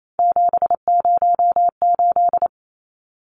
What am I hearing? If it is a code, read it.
Morse code, heard as a keyed tone, about 28 wpm, 700 Hz. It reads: 708
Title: Numbers - 3-Digit 28wpm